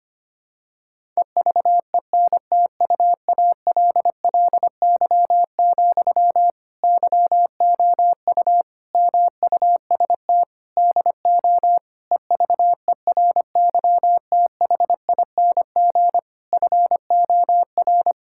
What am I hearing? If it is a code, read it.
Morse code, heard as a keyed tone, about 25 wpm, 700 Hz. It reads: EVENTUALLY, YOU MUST DO EVERYTHING FOR
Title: Sentences from Top 1000 Words - Review Part 6 - 25wpm